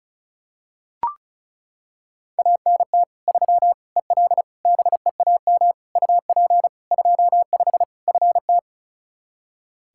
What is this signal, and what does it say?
Morse code, heard as a keyed tone, about 36 wpm, 700 Hz. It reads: ANT 3 EL BEAM UP 25 FT
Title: Practice QSO - Exchange 2 - Info - 35wpm